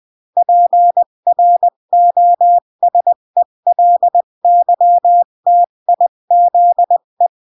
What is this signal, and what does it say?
Morse code, heard as a keyed tone, about 20 wpm, 700 Hz. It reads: PROSELYTIZE
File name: CW / jodorowsky / 82 commandments 20-15